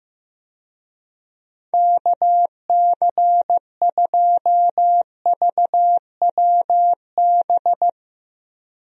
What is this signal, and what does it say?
Morse code, heard as a keyed tone, about 15 wpm, 700 Hz. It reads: KC2VWB
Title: USA Call Signs - 6 Characters - Repeated 3x - 15wpm